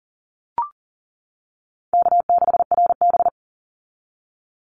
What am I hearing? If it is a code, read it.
Morse code, heard as a keyed tone, about 40 wpm, 700 Hz. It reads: K6RB
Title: Easy Call Signs - 40wpm